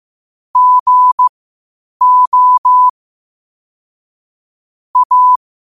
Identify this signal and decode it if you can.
Morse code, heard as a keyed tone, about 15 wpm, spaced as for 8 wpm, 1000 Hz. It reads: GO A
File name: qotdfiles / 8wpm